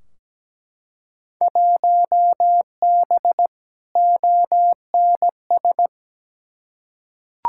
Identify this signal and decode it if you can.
Morse code, heard as a keyed tone, about 17 wpm, 700 Hz. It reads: 1B ONS
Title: Field Day Exchanges Encore - 17wpm